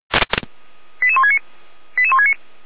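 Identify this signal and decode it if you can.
Five-tone selective call (ZVEI), heard as quick runs of five tones, tones of about 70 ms. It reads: 88169 88169
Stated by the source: Received by SDR as a ZVEI five-tone call.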